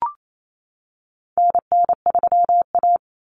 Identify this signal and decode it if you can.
Morse code, heard as a keyed tone, about 28 wpm, 700 Hz. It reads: NN3A
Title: Common Cut Numbers - 4-Digit 28wpm